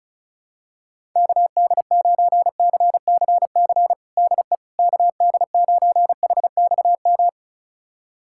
Morse code, heard as a keyed tone, about 34 wpm, 690 Hz. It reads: KD9CCC DE KD9HXM
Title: USA Callsign DE Callsign - 6 Characters - 35wpm